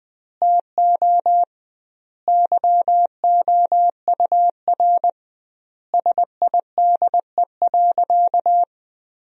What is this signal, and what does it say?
Morse code, heard as a keyed tone, about 21 wpm, 700 Hz. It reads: TO YOUR SIDE.
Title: 2x Word-Spacing - Sentences from Top 400 Words - 20wpm